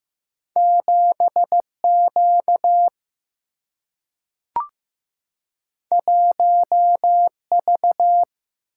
Morse code, heard as a keyed tone, about 15 wpm, 700 Hz. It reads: 7Q 1V
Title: Single-Letter Single-Number 15wpm